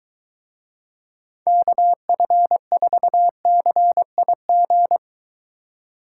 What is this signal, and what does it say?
Morse code, heard as a keyed tone, about 23 wpm, 700 Hz. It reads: KF4CIG